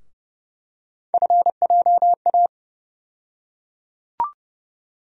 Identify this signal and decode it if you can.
Morse code, heard as a keyed tone, about 30 wpm, 700 Hz. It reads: FJA
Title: USA Call Sign Postfixes - 3 Letters 30wpm